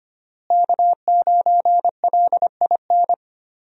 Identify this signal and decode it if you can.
Morse code, heard as a keyed tone, about 25 wpm, 700 Hz. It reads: K9LIN
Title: USA Call Signs - 5 Characters - Repeated 3x - 25wpm